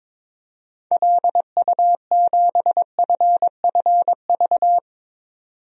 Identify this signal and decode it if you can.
Morse code, heard as a keyed tone, about 21 wpm, 700 Hz. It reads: LU7FFV
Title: DXCC-100 Call Signs - 6 Characters - 22wpm